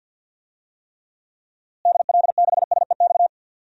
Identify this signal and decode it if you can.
Morse code, heard as a keyed tone, about 50 wpm, 700 Hz. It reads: DL6SEX
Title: DXCC-100 Call Signs - 6 Characters - Repeated 2x - 50wpm